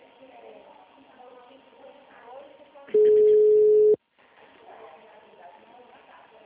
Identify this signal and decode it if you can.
Ringback tone (Europe, 425 Hz), with speech.
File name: out- -2002- - - .10